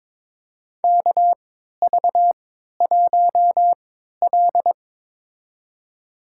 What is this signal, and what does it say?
Morse code, heard as a keyed tone, about 22 wpm, spaced as for 12 wpm, 700 Hz. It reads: KV1L
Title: USA Call Signs - 3x Character Spacing - 22wpm